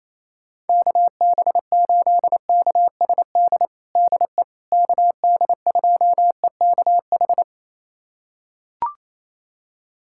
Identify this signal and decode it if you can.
Morse code, heard as a keyed tone, about 28 wpm, 700 Hz. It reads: KB8KSD DE KD2EKH
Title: USA Callsign DE Callsign 28wpm